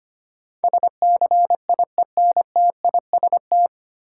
Morse code, heard as a keyed tone, about 25 wpm, 700 Hz. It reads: SCIENTIST
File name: practice / 25wpm / 9 Letter Words - First 3 Letters Given - 25wpm